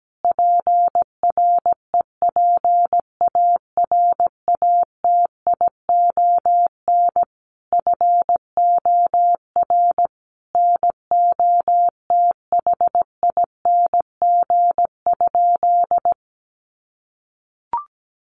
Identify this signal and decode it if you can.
Morse code, heard as a keyed tone, about 17 wpm, 700 Hz. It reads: PREPARATION FOR NOTHING?
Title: Sentences from Top 1400 Words 17wpm